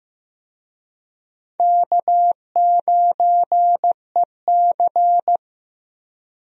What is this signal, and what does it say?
Morse code, heard as a keyed tone, about 15 wpm, 700 Hz. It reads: K9EC
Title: USA Call Signs - 4 Characters - 15wpm